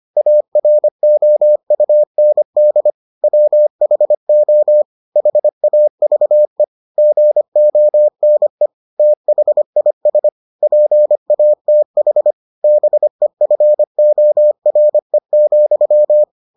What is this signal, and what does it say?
Morse code, heard as a keyed tone, about 25 wpm, 600 Hz. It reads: AROUND WHO HAVE GONE THIS PATH BEFORE,